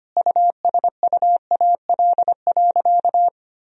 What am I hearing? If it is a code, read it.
Morse code, heard as a keyed tone, about 25 wpm, 700 Hz. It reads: USUAL.
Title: Sentences from Top 1000 Words - Review Part 2 - 25wpm